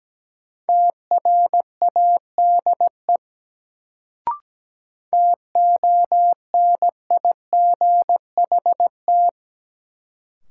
Morse code, heard as a keyed tone, about 17 wpm, 700 Hz. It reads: TRADE TONIGHT
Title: Sentences from Top 600 Words - New Word Drill - 17wpm